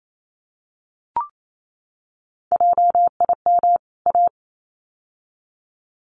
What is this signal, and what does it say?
Morse code, heard as a keyed tone, about 28 wpm, 700 Hz. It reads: JIM A
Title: CWT Exchange 28wpm